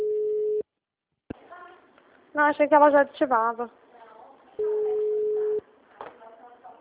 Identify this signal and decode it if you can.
Ringback tone (Europe, 425 Hz), with speech.